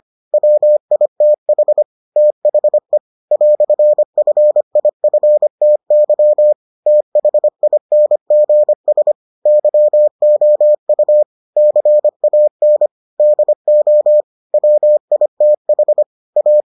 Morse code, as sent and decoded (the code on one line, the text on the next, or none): .-- .. - .... / - .... / .-..-. ..-. .. ..-. - -.-- / - .... .. -. --. ... / -.-- --- ..- / -.-. .- -. / -.. --- / .-- .. - .... / .-
WITH THE "FIFTY THINGS YOU CAN DO WITH A